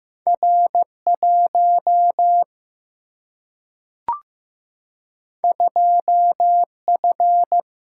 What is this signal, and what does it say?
Morse code, heard as a keyed tone, about 15 wpm, 700 Hz. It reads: R 1 2 F
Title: Single-Letter Single-Number 15wpm